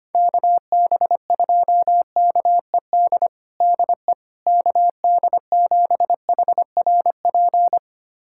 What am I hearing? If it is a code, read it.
Morse code, heard as a keyed tone, about 25 wpm, 700 Hz. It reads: KB2KED DE KD7HRP
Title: USA Callsign DE Callsign 25wpm